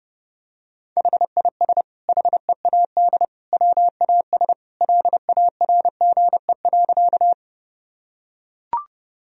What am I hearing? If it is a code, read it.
Morse code, heard as a keyed tone, about 29 wpm, 700 Hz. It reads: HIS HEAD WAS LARGE.
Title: Sentences from Top 1000 Words - Review Part 1 - 30wpm